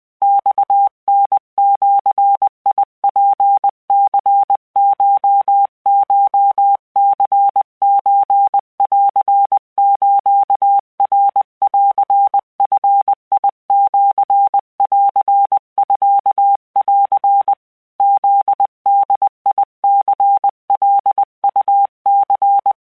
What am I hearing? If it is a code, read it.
Morse code (Japanese Wabun code), heard as a keyed tone, about 20 wpm, 800 Hz. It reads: マタジツニココニソンスナンヂシンミン フボニカウニ
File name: keuikuchiyokugo20wptan